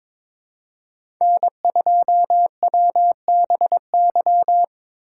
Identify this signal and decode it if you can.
Morse code, heard as a keyed tone, about 22 wpm, 700 Hz. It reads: N2WBY
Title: USA Call Signs - 5 Characters - 6x Character Spacing - 22wpm